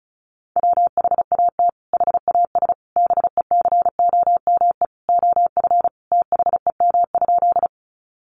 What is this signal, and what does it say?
Morse code, heard as a keyed tone, about 34 wpm, 700 Hz. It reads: WHAT HAS BECOME OF THEM?
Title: Sentences from Top 600 Words 35wpm